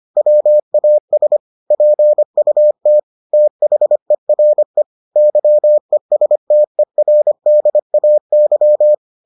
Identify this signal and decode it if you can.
Morse code, heard as a keyed tone, about 25 wpm, 600 Hz. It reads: WAS PUT THERE YESTERDAY